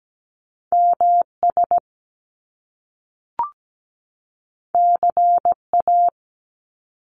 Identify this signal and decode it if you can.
Morse code, heard as a keyed tone, about 17 wpm, 700 Hz. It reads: MS CA
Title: US States 17wpm